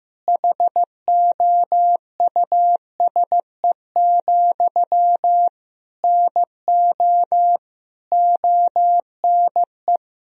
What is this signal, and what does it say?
Morse code, heard as a keyed tone, about 15 wpm, 700 Hz. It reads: HOUSE, NO ONE
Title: Sentences from Top 1000 Words - Review Part 2 - 15wpm